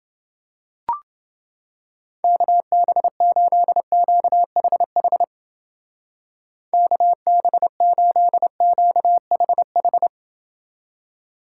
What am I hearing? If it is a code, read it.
Morse code, heard as a keyed tone, about 28 wpm, 700 Hz. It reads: KB8QHH KB8QHH